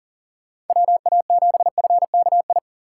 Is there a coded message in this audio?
Yes: Morse code from 0.7 to 2.6 s, heard as a keyed tone, about 52 wpm, 700 Hz. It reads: WA7FKI